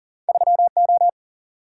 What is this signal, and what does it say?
Morse code, heard as a keyed tone, about 42 wpm, 700 Hz. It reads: 3O
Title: Single-Letter Single-Number 40wpm